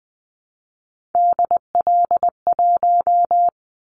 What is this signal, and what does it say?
Morse code, heard as a keyed tone, about 20 wpm, 700 Hz. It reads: DL1
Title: Callsign Prefixes for Top 50 DXCC - 20wpm